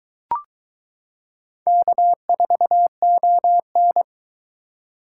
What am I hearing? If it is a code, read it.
Morse code, heard as a keyed tone, about 22 wpm, 700 Hz. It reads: K4ON